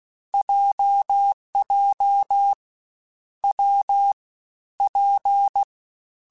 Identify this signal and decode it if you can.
Morse code, heard as a keyed tone, about 16 wpm, 800 Hz. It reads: JJ W P
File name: Difficult Combos / Difficult Combos-mp3s / WPJ WPJ1 @15